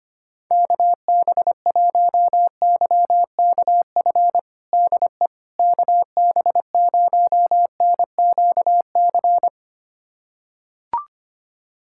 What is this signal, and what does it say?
Morse code, heard as a keyed tone, about 25 wpm, 700 Hz. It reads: KB1YKF DE KB0NQC
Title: USA Callsign DE Callsign - 6 Characters - 25wpm